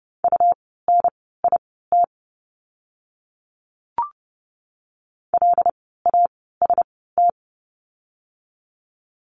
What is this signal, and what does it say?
Morse code, heard as a keyed tone, about 30 wpm, spaced as for 17 wpm, 700 Hz. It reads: UNIT LAST
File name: Lesson 17f - Letter u - Words 30wpm